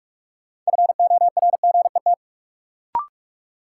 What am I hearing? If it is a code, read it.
Morse code, heard as a keyed tone, about 46 wpm, 700 Hz. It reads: FORGET
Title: Sentences from Top 1200 Words - New Word Drill Alt - 45wpm